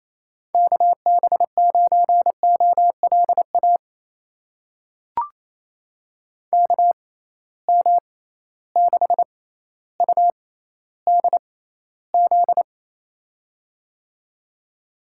Morse code, heard as a keyed tone, about 27 wpm, spaced as for 14 wpm, 700 Hz. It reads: KB9OLA KM6UDZ
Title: USA Call Signs - 6 Characters - 6x Character Spacing - 28wpm